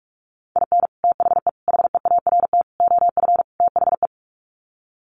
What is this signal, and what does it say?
Morse code, heard as a keyed tone, about 44 wpm, 700 Hz. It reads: IN THE HEART OF THE
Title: Sets of 5 Words 45wpm